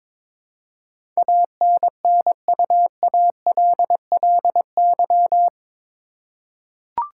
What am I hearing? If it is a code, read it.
Morse code, heard as a keyed tone, about 22 wpm, 700 Hz. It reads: ANNUALLY
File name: practice / 22wpm / Eight Letter Words - 22wpm